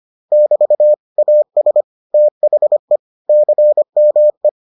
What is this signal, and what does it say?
Morse code, heard as a keyed tone, about 25 wpm, 600 Hz. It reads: <BT> AS THE CME